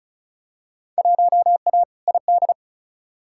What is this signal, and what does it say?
Morse code, heard as a keyed tone, about 36 wpm, 700 Hz. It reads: 1A ID
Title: Field Day Exchanges - 35wpm